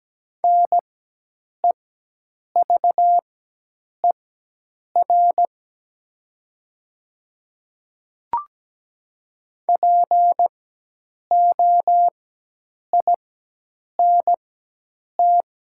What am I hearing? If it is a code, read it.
Morse code, heard as a keyed tone, about 17 wpm, spaced as for 8 wpm, 700 Hz. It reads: NEVER POINT
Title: 4x Character Spacing - Top 300 Words - 17wpm